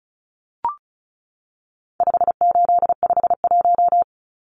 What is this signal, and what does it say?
Morse code, heard as a keyed tone, about 34 wpm, 700 Hz. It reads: 5851
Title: Numbers - 4-Digit 35wpm